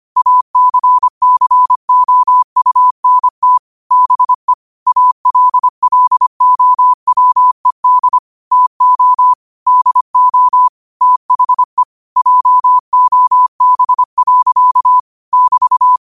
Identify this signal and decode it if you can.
Morse code, heard as a keyed tone, about 25 wpm, 1000 Hz. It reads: ACCOUNT BE ALLOWED TO DO THE JOB. <BT>
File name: qotdfiles / 25wpm